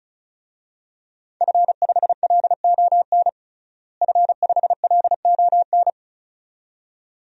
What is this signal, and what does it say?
Morse code, heard as a keyed tone, about 36 wpm, 700 Hz. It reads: F5LON F5LON
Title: DXCC-100 Call Signs - 5 Characters - Repeated 2x - 35wpm